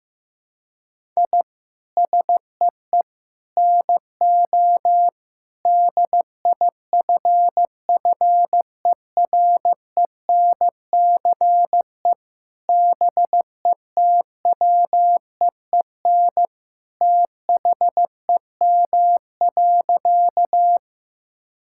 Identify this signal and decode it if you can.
Morse code, heard as a keyed tone, about 16 wpm, 700 Hz. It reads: I SEE NO DIFFERENCE BETWEEN THEM.
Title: Sentences from Top 1000 Words - Review Part 5 - 15wpm